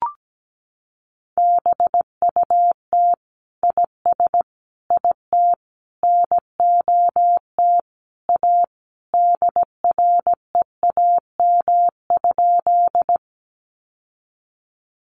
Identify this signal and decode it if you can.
Morse code, heard as a keyed tone, about 17 wpm, 700 Hz. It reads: BUT IS IT NOT A DREAM?
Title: Sentences from Top 1500 Words 17wpm